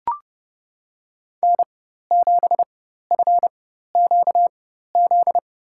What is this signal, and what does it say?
Morse code, heard as a keyed tone, about 30 wpm, spaced as for 14 wpm, 700 Hz. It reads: N7FQZ
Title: USA Call Signs - 5 Characters - 4x Character Spacing - 30wpm